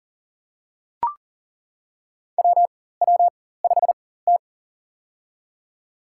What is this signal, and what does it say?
Morse code, heard as a keyed tone, about 40 wpm, spaced as for 19 wpm, 700 Hz. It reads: WW5T
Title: USA Call Signs - 4 Characters - 4x Character Spacing - 40wpm